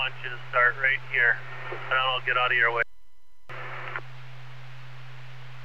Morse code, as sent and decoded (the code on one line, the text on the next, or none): none